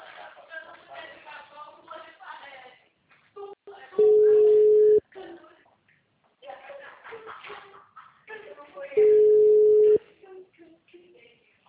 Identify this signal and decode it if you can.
Ringback tone (Europe, 425 Hz), with speech.
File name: out- -2023- - - .827